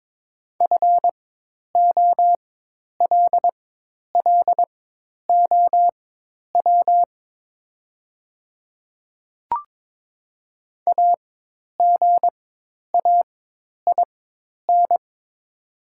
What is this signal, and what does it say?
Morse code, heard as a keyed tone, about 22 wpm, spaced as for 10 wpm, 700 Hz. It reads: FOLLOW AGAIN
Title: 4x Character Spacing - Top 200 Words - 22wpm